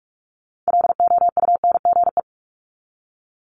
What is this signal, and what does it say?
Morse code, heard as a keyed tone, about 46 wpm, 700 Hz. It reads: LOUNGE
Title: Multi-letter Phonemes 45wpm